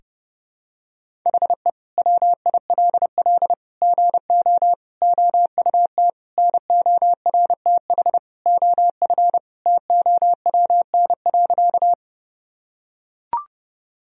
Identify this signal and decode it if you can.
Morse code, heard as a keyed tone, about 30 wpm, 700 Hz. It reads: HE WILL GO OUT NORTH OF TOWN.
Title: Sentences from Top 1000 Words - Review Part 1 - 30wpm